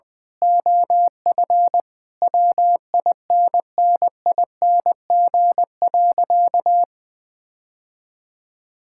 Morse code, heard as a keyed tone, about 20 wpm, 700 Hz. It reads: OF WINNING.
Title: Sentences with 8-Word Phrases - 20wpm